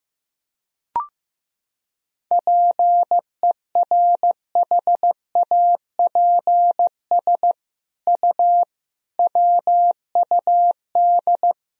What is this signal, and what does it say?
Morse code, heard as a keyed tone, about 15 wpm, 700 Hz. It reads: PERHAPS U WUD
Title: Sentences from Top 500 Words and Abbreviations 15wpm